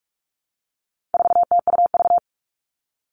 Morse code, heard as a keyed tone, about 44 wpm, 700 Hz. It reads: 4TUV
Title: All Cut Numbers - 4-Digit 45wpm